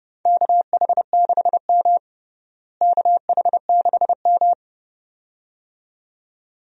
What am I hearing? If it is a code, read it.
Morse code, heard as a keyed tone, about 30 wpm, 700 Hz. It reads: KH6M KH6M
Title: DXCC-100 Call Signs - 4 Characters - Repeated 2x - 30wpm